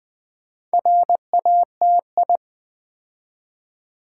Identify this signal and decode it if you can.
Morse code, heard as a keyed tone, about 20 wpm, 700 Hz. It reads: RATI
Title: Build A Word - 2 Letters At A Time - 8-Letter Words - 20wpm